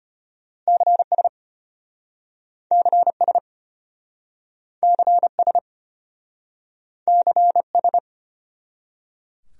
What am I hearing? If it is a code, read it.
Morse code, heard as a keyed tone, about 34 wpm, 700 Hz. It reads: CS CS CS CS